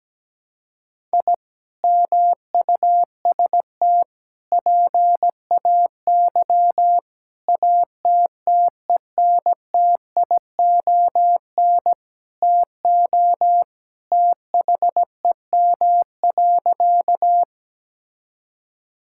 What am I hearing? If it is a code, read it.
Morse code, heard as a keyed tone, about 17 wpm, 700 Hz. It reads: I MUST PAY ATTENTION TO THEM.
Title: Sentences from Top 1000 Words - Review Part 6 - 17wpm